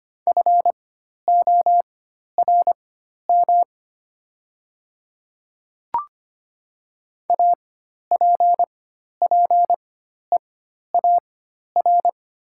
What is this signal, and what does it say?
Morse code, heard as a keyed tone, about 25 wpm, spaced as for 12 wpm, 700 Hz. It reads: FORM APPEAR
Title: 4x Character Spacing - Top 500 Words - 25wpm